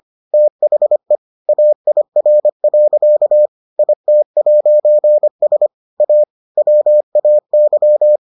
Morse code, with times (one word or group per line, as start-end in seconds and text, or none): 0.3-1.2 THE
1.5-3.5 AIR.
3.8-5.7 IT'S
6.0-6.2 A
6.6-8.2 WAY